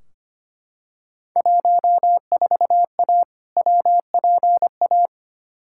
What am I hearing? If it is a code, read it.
Morse code, heard as a keyed tone, about 25 wpm, 700 Hz. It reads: 14A WPA
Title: Field Day Exchanges Encore - 25wpm